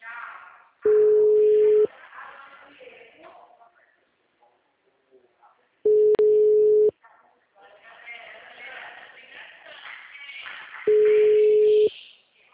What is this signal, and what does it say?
Ringback tone (Europe, 425 Hz), with speech.
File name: out- -2045- - - .12668